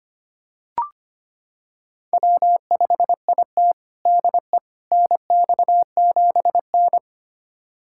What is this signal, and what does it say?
Morse code, heard as a keyed tone, about 25 wpm, 700 Hz. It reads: W5IT DE NX7N
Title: USA Callsign DE Callsign - 4 Characters - 25wpm